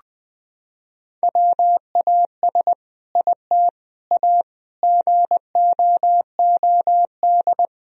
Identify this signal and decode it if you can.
Morse code, heard as a keyed tone, about 20 wpm, 700 Hz. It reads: WAS IT A GOOD
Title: Sentences from Top 1000 Words - Review Part 4 - 20wpm